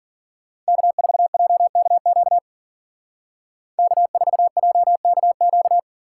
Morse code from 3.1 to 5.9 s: K4JKQ

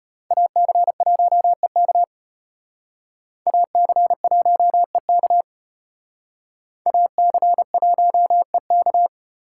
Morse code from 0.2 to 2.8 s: AC1EK

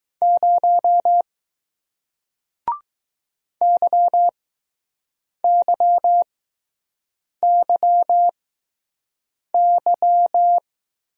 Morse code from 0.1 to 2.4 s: 0

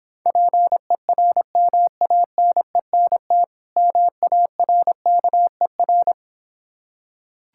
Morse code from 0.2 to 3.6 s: PERMANENT

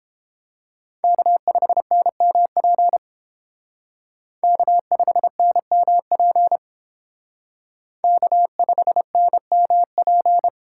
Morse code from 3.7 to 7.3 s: K5NMP

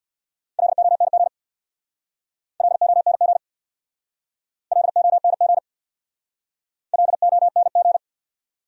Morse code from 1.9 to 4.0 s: LONG

